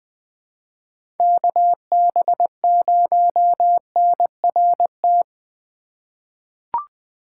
Morse code, heard as a keyed tone, about 20 wpm, 700 Hz. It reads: KB0NRT
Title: USA Call Signs - 6 Characters - 3x Character Spacing - 20wpm